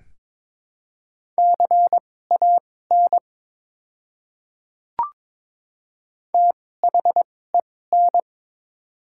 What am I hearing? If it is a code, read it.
Morse code, heard as a keyed tone, about 22 wpm, spaced as for 16 wpm, 700 Hz. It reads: CAN THEN
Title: 2x Character Spacing - Top 100 Words - 22wpm